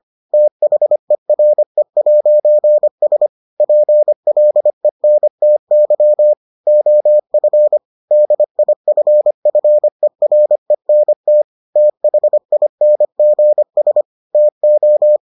Morse code, heard as a keyed tone, about 25 wpm, 600 Hz. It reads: THERE'S PLENTY OF DIFFERENT THINGS TO